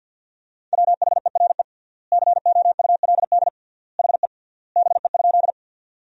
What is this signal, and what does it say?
Morse code, heard as a keyed tone, about 50 wpm, 700 Hz. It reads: WHERE COULD HE BE?
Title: 3x Word-Spacing - Sentences from Top 200 Words - 50wpm